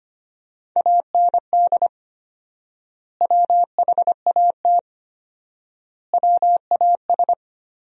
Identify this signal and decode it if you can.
Morse code, heard as a keyed tone, about 25 wpm, 700 Hz. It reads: AND WHAT WAS
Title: 4x Word-Spacing - Sentences from Top 100 Words - 25wpm